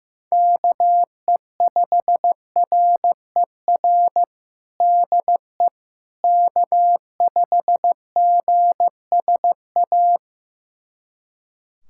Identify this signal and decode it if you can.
Morse code, heard as a keyed tone, about 15 wpm, 700 Hz. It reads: KE5RER DE K5GSA
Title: USA Callsign DE Callsign 15wpm